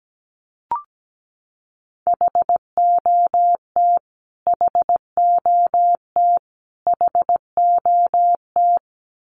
Morse code, heard as a keyed tone, about 17 wpm, 700 Hz. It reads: HOT HOT HOT
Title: Top 300 Words Repeated 17wpm